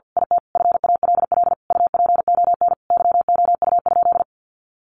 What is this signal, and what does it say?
Morse code, heard as a keyed tone, about 50 wpm, 700 Hz. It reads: IT FALL UPON YOU?
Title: Sentences from Top 1100 Words 50wpm